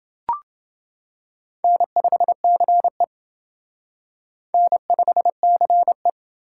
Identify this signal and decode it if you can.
Morse code, heard as a keyed tone, about 28 wpm, 700 Hz. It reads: N5CE N5CE